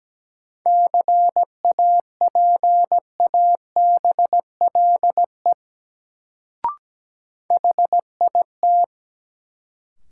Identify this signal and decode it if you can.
Morse code, heard as a keyed tone, about 17 wpm, 700 Hz. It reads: CAPABLE HIT
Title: Sentences from Top 1200 Words - New Word Drill - 17wpm